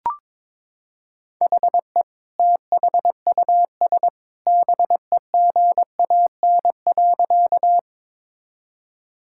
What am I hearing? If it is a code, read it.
Morse code, heard as a keyed tone, about 22 wpm, 700 Hz. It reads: HE THUS BEGAN.
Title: Sentences from Top 1000 Words - Review Part 5 - 22wpm